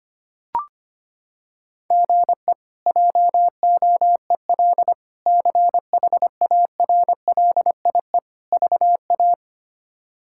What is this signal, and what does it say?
Morse code, heard as a keyed tone, about 25 wpm, 700 Hz. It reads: GE JOEL CHARLIE VA
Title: K1USN SST QSO Practice - 25wpm